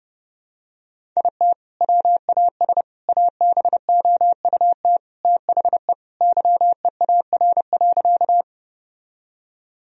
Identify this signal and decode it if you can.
Morse code, heard as a keyed tone, about 30 wpm, 700 Hz. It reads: IT WAS ABOUT THE YEAR.
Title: Sentences from Top 1000 Words - Review Part 1 - 30wpm